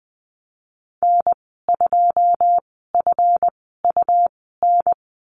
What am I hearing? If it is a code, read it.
Morse code, heard as a keyed tone, about 20 wpm, spaced as for 15 wpm, 700 Hz. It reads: N2FUN
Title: 173 Lesson 28g - Number 9 - Call Signs 20wpm